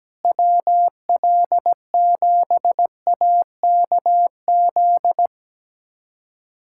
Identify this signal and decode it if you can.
Morse code, heard as a keyed tone, about 17 wpm, 700 Hz. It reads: WL7AKZ
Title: USA Callsign DE Callsign - 6 Characters - 17wpm